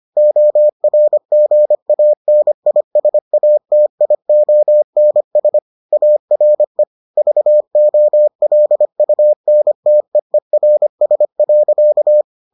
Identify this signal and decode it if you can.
Morse code, heard as a keyed tone, about 25 wpm, 600 Hz. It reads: ORGANISATIONS ARE VOLUNTEERS.